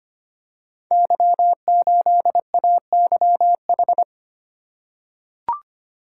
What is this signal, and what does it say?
Morse code, heard as a keyed tone, about 25 wpm, 700 Hz. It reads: Y8AYH
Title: Code Groups - 5 Letter:Numbers - 25wpm